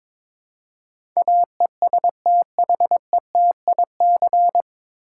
Morse code, heard as a keyed tone, about 22 wpm, 700 Hz. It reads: AESTHETIC
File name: 9 Letter Words - First 3 Letters Given - 22wpm